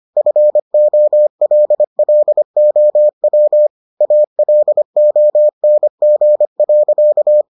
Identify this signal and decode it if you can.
Morse code, heard as a keyed tone, about 25 wpm, 600 Hz. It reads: FOLLOW ALONG.